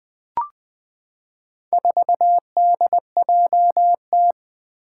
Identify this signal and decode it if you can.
Morse code, heard as a keyed tone, about 20 wpm, 700 Hz. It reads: 4DJT
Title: Four Challenging Letter-Number Combinations 20wpm